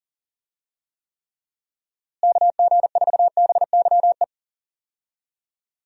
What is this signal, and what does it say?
Morse code, heard as a keyed tone, about 40 wpm, 700 Hz. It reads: KG4BYE
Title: USA Call Signs - 6 Characters - Repeated 3x - 40wpm